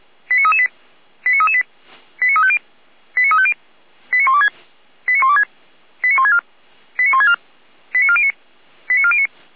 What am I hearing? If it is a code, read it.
Five-tone selective call (ZVEI), heard as quick runs of five tones, tones of about 70 ms. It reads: 89338 89338 89350 89350 89126 89126 89164 89164 89409 89409